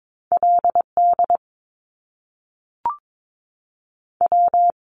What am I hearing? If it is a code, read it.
Morse code, heard as a keyed tone, about 22 wpm, 700 Hz. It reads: LD W
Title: Top Two Letter Combinations in Words 22wpm